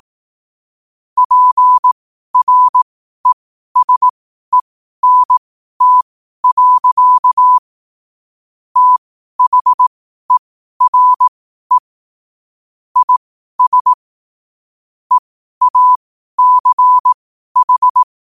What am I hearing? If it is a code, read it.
Morse code, heard as a keyed tone, about 18 wpm, spaced as for 12 wpm, 1000 Hz. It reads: PRESENT. THERE IS EACH
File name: qotdfiles / 12wpm